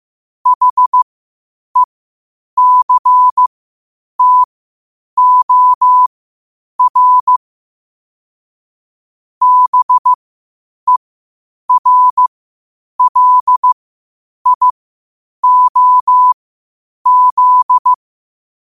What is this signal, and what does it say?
Morse code, heard as a keyed tone, about 15 wpm, spaced as for 8 wpm, 1000 Hz. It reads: HECTOR BERLIOZ
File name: qotdfiles / 8wpm